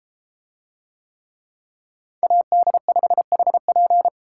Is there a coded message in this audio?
Yes: Morse code from 2.2 to 4.1 s, heard as a keyed tone, about 33 wpm, 700 Hz. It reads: AD5HP